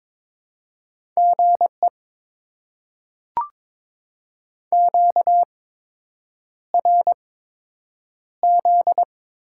Morse code, heard as a keyed tone, about 22 wpm, spaced as for 8 wpm, 700 Hz. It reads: GE QRZ
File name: Top 90 QSO Elements - 8x Character Spacing - 22wpm